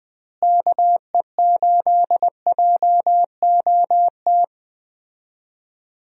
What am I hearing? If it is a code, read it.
Morse code, heard as a keyed tone, about 20 wpm, 700 Hz. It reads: KE8JOT
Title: USA Call Signs - 6 Characters - Repeated 3x - 20wpm